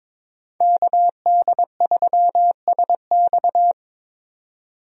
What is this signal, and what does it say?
Morse code, heard as a keyed tone, about 22 wpm, 700 Hz. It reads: KD3SX
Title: USA Call Signs - 5 Characters - 22wpm